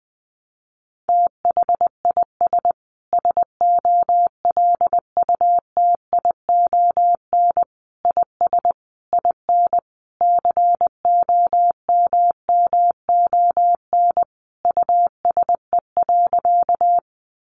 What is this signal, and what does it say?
Morse code, heard as a keyed tone, about 20 wpm, 700 Hz. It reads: THIS SOLUTION IS IN COMMON USE.
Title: Sentences from Top 1300 Words 20wpm